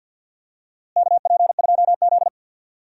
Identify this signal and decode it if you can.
Morse code, heard as a keyed tone, about 51 wpm, 700 Hz. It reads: KP2Z
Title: DXCC-100 Call Signs - 4 Characters - Repeated 3x - 50wpm